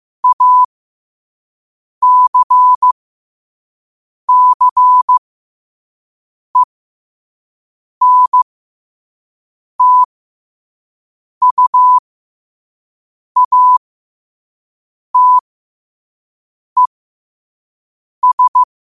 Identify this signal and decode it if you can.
Morse code, heard as a keyed tone, about 15 wpm, spaced as for 5 wpm, 1000 Hz. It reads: ACCENTUATES
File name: qotdfiles / 5wpm